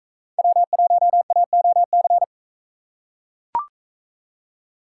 Morse code, heard as a keyed tone, about 39 wpm, 700 Hz. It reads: W1AOC